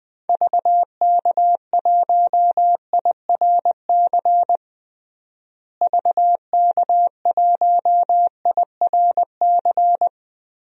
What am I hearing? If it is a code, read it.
Morse code, heard as a keyed tone, about 20 wpm, 700 Hz. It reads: VK1IRC VK1IRC
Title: DXCC-100 Call Signs - 6 Characters - Repeated 2x - 20wpm